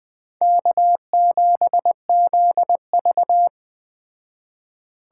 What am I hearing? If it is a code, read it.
Morse code, heard as a keyed tone, about 20 wpm, 700 Hz. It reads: K7ZV